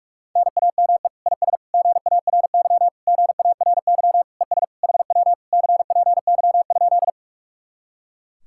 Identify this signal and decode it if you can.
Morse code, heard as a keyed tone, about 57 wpm, 700 Hz. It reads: NAME IS GARY GARY ES HW CPY?